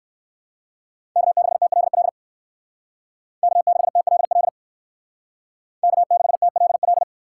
Morse code, heard as a keyed tone, about 58 wpm, 700 Hz. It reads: K6TLL K6TLL K6TLL